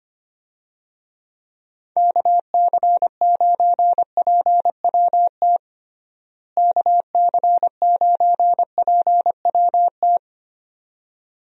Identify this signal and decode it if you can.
Morse code, heard as a keyed tone, about 25 wpm, 700 Hz. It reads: KC9PWT KC9PWT
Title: USA Call Signs - 6 Characters - Repeated 2x - 25wpm